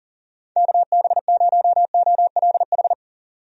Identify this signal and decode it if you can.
Morse code, heard as a keyed tone, about 40 wpm, 700 Hz. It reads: KB0OLH
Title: USA Call Signs - 6 Characters - 40wpm